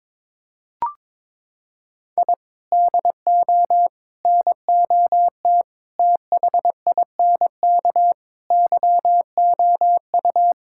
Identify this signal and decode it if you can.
Morse code, heard as a keyed tone, about 21 wpm, 700 Hz. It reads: I DO NOT THINK YOU
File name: practice / 22wpm / Sentences with 4-Word Phrases - 22wpm